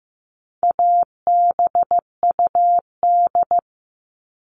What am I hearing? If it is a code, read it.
Morse code, heard as a keyed tone, about 15 wpm, 700 Hz. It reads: ABUD
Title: All Cut Numbers - 4-Digit 15wpm